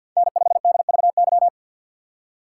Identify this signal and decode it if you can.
Morse code, heard as a keyed tone, about 52 wpm, 700 Hz. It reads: N5DVY
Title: USA Call Signs - 5 Characters - 6x Character Spacing - 50wpm